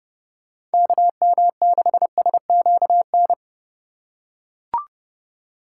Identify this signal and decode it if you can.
Morse code, heard as a keyed tone, about 31 wpm, 700 Hz. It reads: KM6SQN
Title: USA Call Signs - 6 Characters - 2x Character Spacing - 30wpm